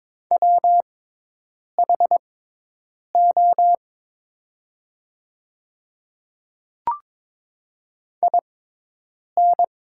Morse code, heard as a keyed tone, about 22 wpm, spaced as for 8 wpm, 700 Hz. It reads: WHO IN
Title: 6x Character Spacing - Top 200 Words - 22wpm